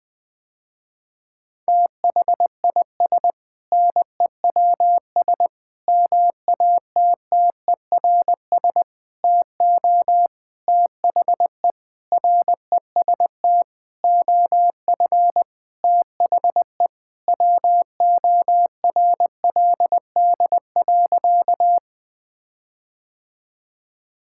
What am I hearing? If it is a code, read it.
Morse code, heard as a keyed tone, about 20 wpm, 700 Hz. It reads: THIS NEWS MATTERS TO THE REST OF THE WORLD.
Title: Sentences with 6-Word Phrases - Spoken Beginning - 20wpm